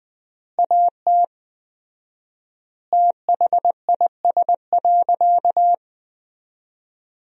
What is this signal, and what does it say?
Morse code, heard as a keyed tone, about 20 wpm, 700 Hz. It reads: AT THIS.
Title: 4x Word-Spacing - Sentences from Top 300 Words - 20wpm